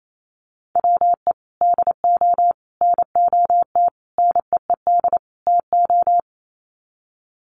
Morse code, heard as a keyed tone, about 28 wpm, 700 Hz. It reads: WE DO NOT NEED TO
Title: Sets of 5 Words Encore 28wpm